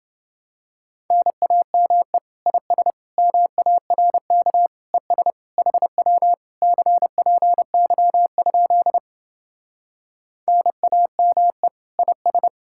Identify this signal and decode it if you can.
Morse code, heard as a keyed tone, about 29 wpm, 700 Hz. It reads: NAME IS MARK ES HW CPY? NAME IS